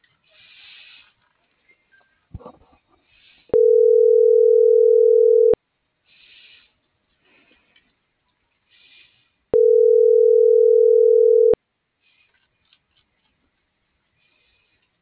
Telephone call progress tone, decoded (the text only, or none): ringback tone